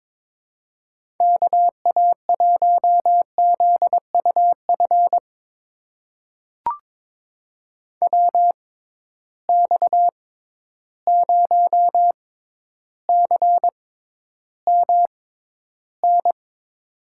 Morse code, heard as a keyed tone, about 22 wpm, spaced as for 11 wpm, 700 Hz. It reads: KA1ZUF WX0CMN
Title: USA Call Signs - 6 Characters - 6x Character Spacing - 22wpm